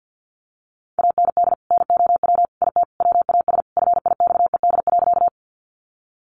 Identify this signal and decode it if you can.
Morse code, heard as a keyed tone, about 50 wpm, 700 Hz. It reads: AND NOW IT WAS FIXED.
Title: Sentences from Top 1100 Words 50wpm